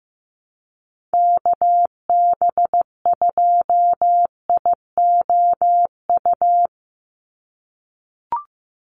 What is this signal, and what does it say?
Morse code, heard as a keyed tone, about 15 wpm, 700 Hz. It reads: KB2IOU
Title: USA Call Signs 15wpm